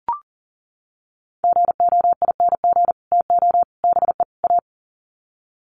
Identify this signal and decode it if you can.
Morse code, heard as a keyed tone, about 40 wpm, 700 Hz. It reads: GOING TO BE A
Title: Sets of 4 Words 40wpm